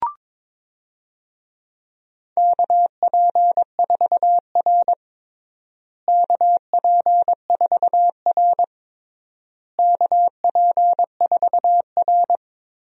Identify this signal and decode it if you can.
Morse code, heard as a keyed tone, about 22 wpm, 700 Hz. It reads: KP4R KP4R KP4R
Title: DXCC-100 Call Signs - 4 Characters - Repeated 3x - 22wpm